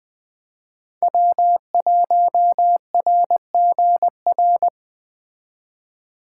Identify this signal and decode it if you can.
Morse code, heard as a keyed tone, about 20 wpm, 700 Hz. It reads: W1RGR
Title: Speed Racing - USA 1x3 Call Signs 17wpm